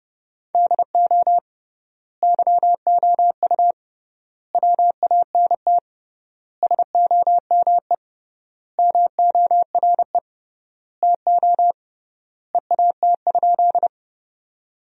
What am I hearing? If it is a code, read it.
Morse code, heard as a keyed tone, about 30 wpm, 700 Hz. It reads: DO YOU WANT SOME MORE TO EAT?
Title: 3x Word-Spacing - Sentences from Top 300 Words - 30wpm